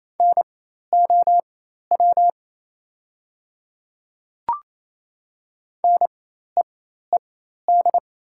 Morse code, heard as a keyed tone, about 28 wpm, spaced as for 13 wpm, 700 Hz. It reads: NOW NEED
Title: 4x Character Spacing - Top 200 Words - 28wpm